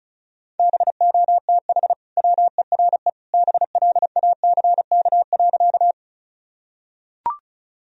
Morse code, heard as a keyed tone, about 34 wpm, 700 Hz. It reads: BOTH WERE BLACK.
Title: Sentences from Top 1000 Words - Review Part 2 - 35wpm